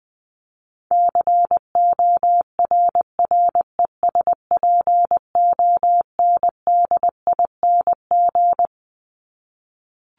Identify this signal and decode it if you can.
Morse code, heard as a keyed tone, about 20 wpm, 700 Hz. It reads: CORRESPONDING
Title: ING Suffix - Top 100 Words with Base Words 20wpm